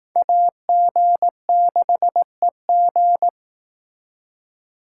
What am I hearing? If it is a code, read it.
Morse code, heard as a keyed tone, about 20 wpm, 700 Hz. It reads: AG6EG